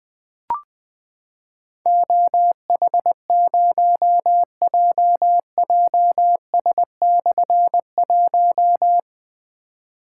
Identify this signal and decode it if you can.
Morse code, heard as a keyed tone, about 20 wpm, 700 Hz. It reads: OH0JJS/1
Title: Difficult Call Signs 20wpm